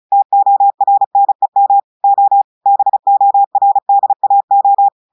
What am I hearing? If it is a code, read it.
Morse code, heard as a keyed tone, about 35 wpm, 800 Hz. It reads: TORNEM O BORDAO
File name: portugais texte en clair 2 vitesse 35 mots minutes